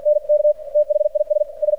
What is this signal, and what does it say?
Morse code, heard as a keyed tone, about 53 wpm, 600 Hz. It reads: N K THIS S
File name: archived / CRSnet